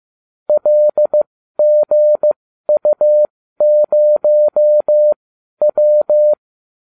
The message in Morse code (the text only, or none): LGU0W